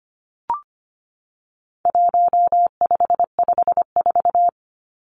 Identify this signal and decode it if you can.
Morse code, heard as a keyed tone, about 25 wpm, 700 Hz. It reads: 1554
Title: Numbers - 4-Digit 25wpm